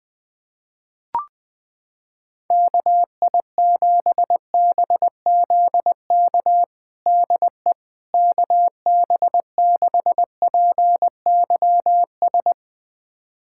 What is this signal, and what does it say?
Morse code, heard as a keyed tone, about 20 wpm, 700 Hz. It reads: KI7BZK DE KB6PYS
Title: USA Callsign DE Callsign - 6 Characters - 20wpm